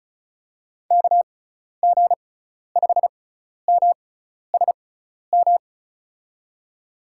Morse code, heard as a keyed tone, about 35 wpm, spaced as for 12 wpm, 700 Hz. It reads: KG5MSM
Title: USA Call Signs - 6 Characters - 6x Character Spacing - 35wpm